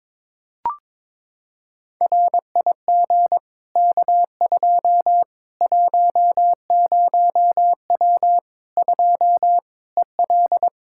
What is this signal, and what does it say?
Morse code, heard as a keyed tone, about 21 wpm, 700 Hz. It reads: RIG K2 10W 2 EL
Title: Practice QSO - Exchange 2 - Info - 22wpm